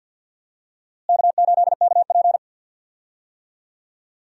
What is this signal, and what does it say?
Morse code, heard as a keyed tone, about 52 wpm, 700 Hz. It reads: K8KP
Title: USA Call Signs - 4 Characters - Repeated 3x - 50wpm